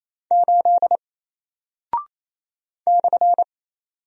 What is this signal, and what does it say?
Morse code, heard as a keyed tone, about 28 wpm, 700 Hz. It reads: 8 /
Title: Single Character (Letter:Number:Punctuation:Prosign) Alt 28wpm